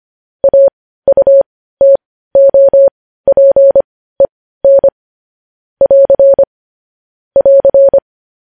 Morse code, heard as a keyed tone, about 25 wpm, spaced as for 15 wpm, 550 Hz. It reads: AUTOPEN <AR> <AR>
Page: news headlines in morse code at 15 wpm